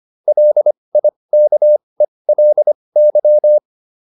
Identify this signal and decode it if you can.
Morse code, heard as a keyed tone, about 25 wpm, spaced as for 20 wpm, 600 Hz. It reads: LIKELY
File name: LY6 20